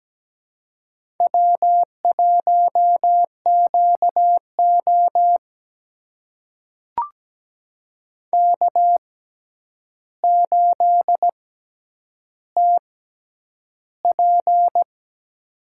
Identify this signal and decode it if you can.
Morse code, heard as a keyed tone, about 17 wpm, spaced as for 9 wpm, 700 Hz. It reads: W1QO K8TP
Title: USA Call Signs - 4 Characters - 6x Character Spacing - 17wpm